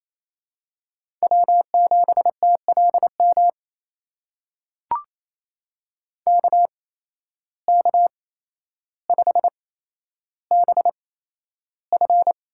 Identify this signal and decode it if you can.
Morse code, heard as a keyed tone, about 28 wpm, spaced as for 12 wpm, 700 Hz. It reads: W7TLM KK5BF
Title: USA Call Signs - 5 Characters - 8x Character Spacing - 28wpm